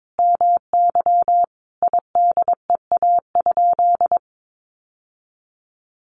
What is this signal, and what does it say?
Morse code, heard as a keyed tone, about 22 wpm, 700 Hz. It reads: MY IDEA?
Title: Sentences from Top 1500 Words 22wpm